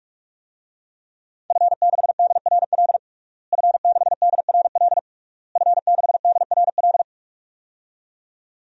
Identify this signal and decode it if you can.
Morse code, heard as a keyed tone, about 46 wpm, 700 Hz. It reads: F6DRL F6DRL F6DRL
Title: DXCC-100 Call Signs - 5 Characters - Repeated 3x - 45wpm